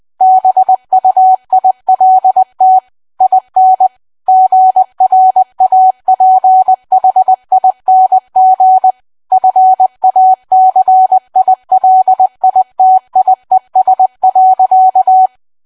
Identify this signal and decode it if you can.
Morse code, heard as a keyed tone, about 20 wpm, 750 Hz. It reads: BUILT IN GRAPHING FACILITIES.